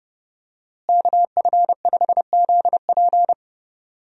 Morse code, heard as a keyed tone, about 30 wpm, 700 Hz. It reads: KF5ZP
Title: USA Call Signs - 5 Characters - Repeated 3x - 30wpm